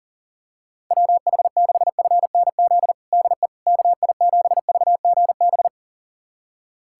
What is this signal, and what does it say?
Morse code, heard as a keyed tone, about 40 wpm, 700 Hz. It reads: WH6FNZ DE KI7VGB
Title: USA Callsign DE Callsign - 6 Characters - 40wpm